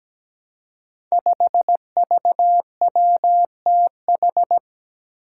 Morse code, heard as a keyed tone, about 17 wpm, 700 Hz. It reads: 5VWTH
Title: Five Challenging Letter-Number Combinations 17wpm